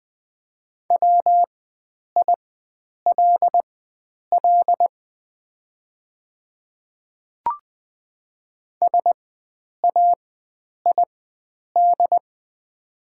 Morse code, heard as a keyed tone, about 20 wpm, spaced as for 9 wpm, 690 Hz. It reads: WILL SAID
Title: 4x Character Spacing - Top 100 Words - 20wpm